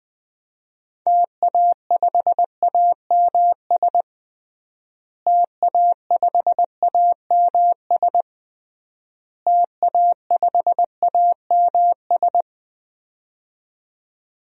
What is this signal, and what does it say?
Morse code, heard as a keyed tone, about 21 wpm, 700 Hz. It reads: TA5AMS TA5AMS TA5AMS
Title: DXCC-100 Call Signs - 6 Characters - Repeated 3x - 20wpm